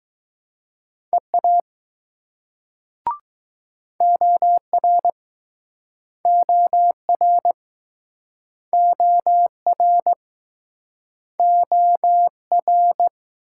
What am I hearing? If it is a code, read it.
Morse code, heard as a keyed tone, about 22 wpm, 700 Hz. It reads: EA OR OR OR OR